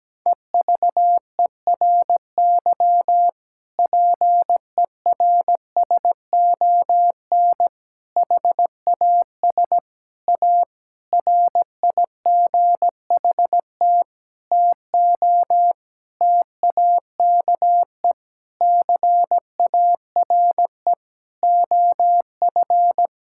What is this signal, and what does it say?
Morse code, heard as a keyed tone, about 17 wpm, 700 Hz. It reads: EVERY PERSON HAS A RIGHT TO TAKE CARE OF
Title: Sentences from Top 1000 Words - Review Part 5 - 17wpm